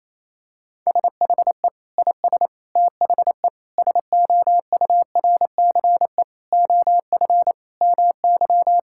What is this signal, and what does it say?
Morse code, heard as a keyed tone, about 28 wpm, 700 Hz. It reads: SHE IS THE SOURCE OF MY
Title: Sentences from Top 1000 Words - Review Part 5 - 28wpm